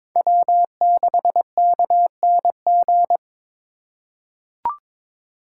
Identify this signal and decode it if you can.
Morse code, heard as a keyed tone, about 22 wpm, 700 Hz. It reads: W6KNG
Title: USA Call Signs - 5 Characters - 2x Character Spacing - 22wpm